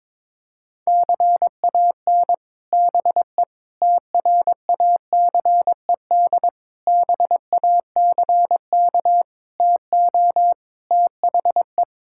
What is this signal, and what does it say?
Morse code, heard as a keyed tone, about 22 wpm, 700 Hz. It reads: CAN BE TRACED BACK TO THE
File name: practice / 22wpm / Sets of 6 Words - 22wpm